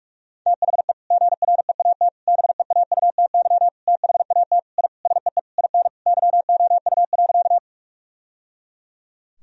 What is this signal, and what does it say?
Morse code, heard as a keyed tone, about 42 wpm, 700 Hz. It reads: THE GREAT BEAUTY THAT I SEE IN YOU.
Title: Sentences from Top 1000 Words - Review Part 2 - 45wpm